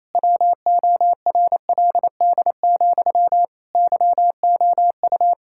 Morse code, heard as a keyed tone, about 27 wpm, 700 Hz. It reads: WORLD, YOU
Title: Sentences from Top 1000 Words - Review Part 6 - 28wpm